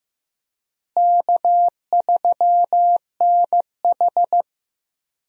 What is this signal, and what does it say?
Morse code, heard as a keyed tone, about 15 wpm, 700 Hz. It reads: K3NH
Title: USA Call Signs - 4 Characters - Repeated 3x - 15wpm